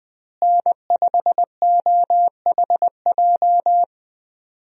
Morse code, heard as a keyed tone, about 21 wpm, 700 Hz. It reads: N5OHJ
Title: USA Call Signs - 5 Characters - Repeated 3x - 20wpm